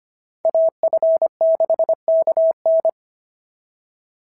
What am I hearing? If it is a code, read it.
Morse code, heard as a keyed tone, about 25 wpm, 650 Hz. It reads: AF6KN
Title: SOTA QSO Practice 25wpm